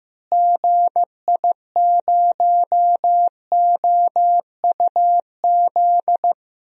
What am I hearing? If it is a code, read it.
Morse code, heard as a keyed tone, about 15 wpm, 700 Hz. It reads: GI0OUZ
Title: DXCC-100 Call Signs - 6 Characters - Repeated 3x - 15wpm